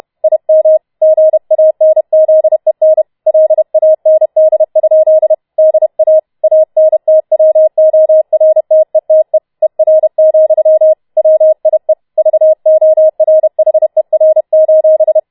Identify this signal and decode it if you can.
Morse code, heard as a keyed tone, about 31 wpm, 630 Hz. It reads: IM GANZEN LAND? DA ANTWORTETE ER, WIE VORHER: